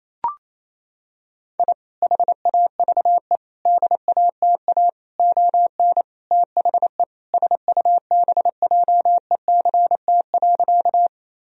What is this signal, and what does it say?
Morse code, heard as a keyed tone, about 27 wpm, 700 Hz. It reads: I HAVE DATA ON THE SUBJECT.
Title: Sentences from Top 1000 Words - Review Part 3 - 28wpm